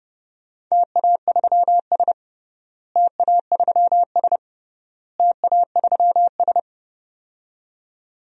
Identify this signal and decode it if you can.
Morse code, heard as a keyed tone, about 30 wpm, 700 Hz. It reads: TA3S TA3S TA3S
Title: DXCC-100 Call Signs - 4 Characters - Repeated 3x - 30wpm